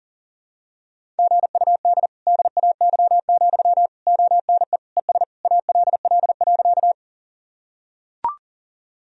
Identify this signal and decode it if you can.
Morse code, heard as a keyed tone, about 40 wpm, 700 Hz. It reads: GUD DAY, ONE ES ALL.
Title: Sentences from Top 200 Words and Abbreviations 40wpm